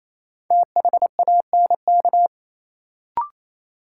Morse code, heard as a keyed tone, about 28 wpm, 700 Hz. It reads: THANK
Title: Sets of 2 Words - New Word Drill - 28wpm